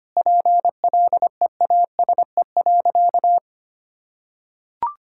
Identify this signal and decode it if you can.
Morse code, heard as a keyed tone, about 25 wpm, 700 Hz. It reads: PLEASE.
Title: Sentences from Top 1600 Words 25wpm